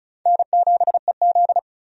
Morse code, heard as a keyed tone, about 36 wpm, 700 Hz. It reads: N7EZ
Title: USA Call Signs - 4 Characters - 35wpm